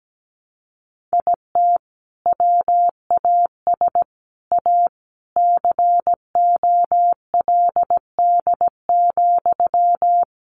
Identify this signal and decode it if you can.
Morse code, heard as a keyed tone, about 17 wpm, 700 Hz. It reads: IT WAS A COLD,
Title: Sentences from Top 500 Words 17wpm